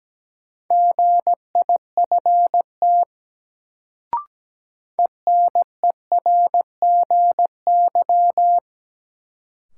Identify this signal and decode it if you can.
Morse code, heard as a keyed tone, about 17 wpm, 700 Hz. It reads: GIFT ENERGY
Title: Sentences from Top 900 Words - New Word Drill - 17wpm